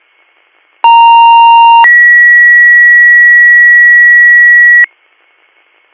Two-tone page, heard as a one-second tone, then a three-second tone, about 930 then 1950 Hz. It is heard from 0.8 to 4.9 s.